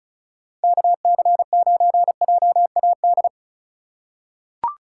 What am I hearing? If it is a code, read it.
Morse code, heard as a keyed tone, about 36 wpm, 690 Hz. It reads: KC9JAD
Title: USA Call Signs - 6 Characters - 4x Character Spacing - 35wpm